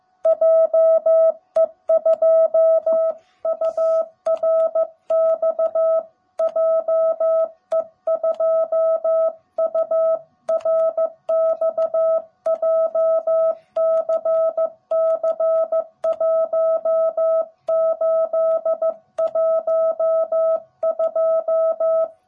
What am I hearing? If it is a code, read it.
Morse code, heard as a keyed tone, about 15 wpm, 660 Hz. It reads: JE2URX JE2URX JCC1812
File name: jh8cbh / 2021cw / cw18b